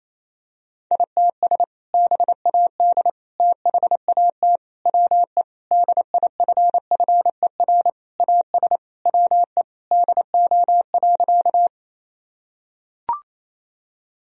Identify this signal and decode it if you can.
Morse code, heard as a keyed tone, about 28 wpm, 700 Hz. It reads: ITS BAD THAT WE DIFFER AS WE DO.
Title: Sentences from Top 1000 Words - Review Part 1 - 28wpm